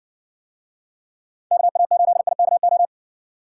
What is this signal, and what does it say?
Morse code, heard as a keyed tone, about 54 wpm, 700 Hz. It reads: KA8IKO